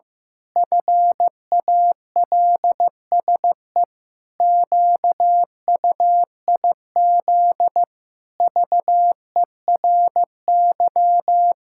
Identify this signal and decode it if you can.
Morse code, heard as a keyed tone, about 15 wpm, 700 Hz. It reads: FALSE QUIZ VERY
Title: Pangram Challenge v2 - 15wpm